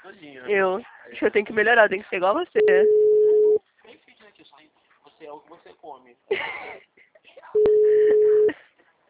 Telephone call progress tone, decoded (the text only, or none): ringback tone